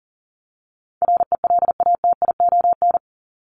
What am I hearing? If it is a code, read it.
Morse code, heard as a keyed tone, about 40 wpm, 700 Hz. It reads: RELATION
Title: ES:S Suffix - Top 100 Words with Base Words 40wpm